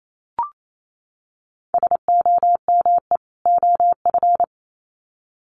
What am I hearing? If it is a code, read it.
Morse code, heard as a keyed tone, about 28 wpm, 700 Hz. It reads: SOME OF
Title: Sets of 2 Words 28wpm